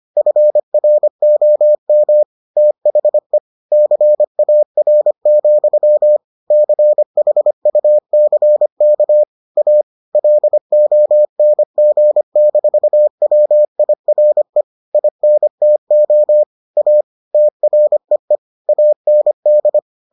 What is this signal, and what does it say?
Morse code, heard as a keyed tone, about 25 wpm, 600 Hz. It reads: FROM THE CAR, CHUCK A LONG-WIRE INTO A TREE AND